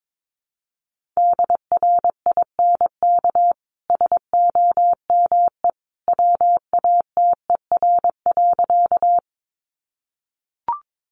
Morse code, heard as a keyed tone, about 22 wpm, 700 Hz. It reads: DRINK SOME WATER.
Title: Sentences from Top 1500 Words 22wpm